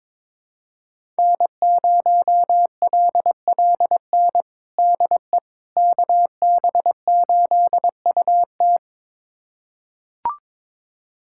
Morse code, heard as a keyed tone, about 22 wpm, 700 Hz. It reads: N0LLN DE KB8UT
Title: USA Callsign DE Callsign - 5 Characters - 22wpm